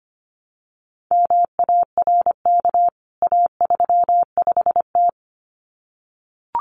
Morse code, heard as a keyed tone, about 25 wpm, 700 Hz. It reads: MARK A35T
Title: CWT Exchange 25wpm